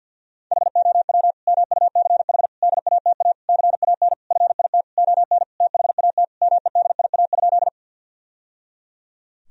Morse code, heard as a keyed tone, about 50 wpm, 700 Hz. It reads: HOW MUCH DATA CAN FIT ON THAT MEDIA?
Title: Sentences from Top 1000 Words - Review Part 4 - 50wpm